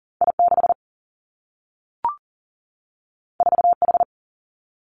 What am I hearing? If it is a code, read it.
Morse code, heard as a keyed tone, about 40 wpm, 700 Hz. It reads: I 6 4 H
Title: Two Challenging Letter-Number Combinations 40wpm